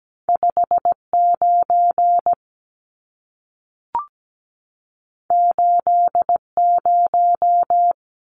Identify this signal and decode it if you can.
Morse code, heard as a keyed tone, about 17 wpm, 700 Hz. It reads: 59 80
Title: Numbers - 2-Digit 17wpm